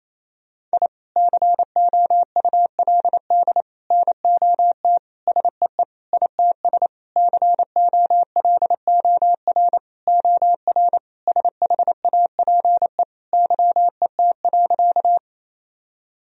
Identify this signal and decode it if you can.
Morse code, heard as a keyed tone, about 28 wpm, 700 Hz. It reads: I COULD NOT SEE ITS COLOR OR SHAPE YET.
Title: Sentences from Top 1000 Words - Review Part 3 - 28wpm